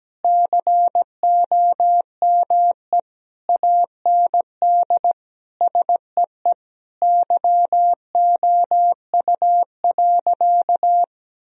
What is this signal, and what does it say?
Morse code, heard as a keyed tone, about 17 wpm, 700 Hz. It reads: COME AND SEE YOU.
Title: Sentences from Top 1000 Words - Review Part 4 - 17wpm